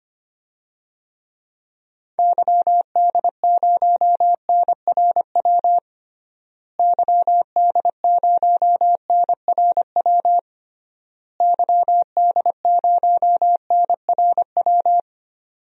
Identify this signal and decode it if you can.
Morse code, heard as a keyed tone, about 25 wpm, 700 Hz. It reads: YD0NRW YD0NRW YD0NRW
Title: DXCC-100 Call Signs - 6 Characters - Repeated 3x - 25wpm